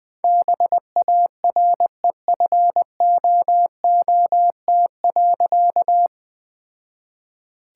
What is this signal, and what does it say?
Morse code, heard as a keyed tone, about 21 wpm, 700 Hz. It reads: BAREFOOT.
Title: All-Age Appropriate Jokes - 20wpm